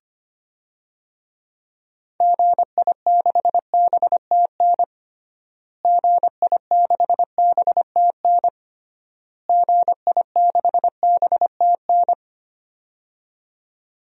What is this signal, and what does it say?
Morse code, heard as a keyed tone, about 25 wpm, 700 Hz. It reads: GI6BTN GI6BTN GI6BTN
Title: DXCC-100 Call Signs - 6 Characters - Repeated 3x - 25wpm